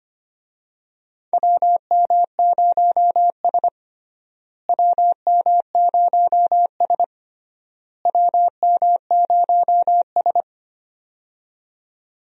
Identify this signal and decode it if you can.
Morse code, heard as a keyed tone, about 25 wpm, 700 Hz. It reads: WM0S WM0S WM0S
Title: USA Call Signs - 4 Characters - Repeated 3x - 25wpm